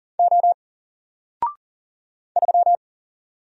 Morse code, heard as a keyed tone, about 40 wpm, 700 Hz. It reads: O 3
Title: Single Character (Letter:Number:Punctuation:Prosign) Alt 40wpm